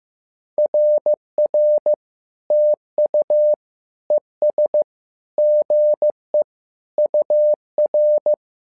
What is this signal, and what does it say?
Morse code, heard as a keyed tone, about 15 wpm, 600 Hz. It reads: RR TU ES GE UR
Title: SOTA QSO Practice 15wpm